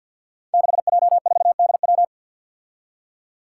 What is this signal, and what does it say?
Morse code, heard as a keyed tone, about 52 wpm, 700 Hz. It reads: 6J4DW
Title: Five Challenging Letter-Number Combinations 50wpm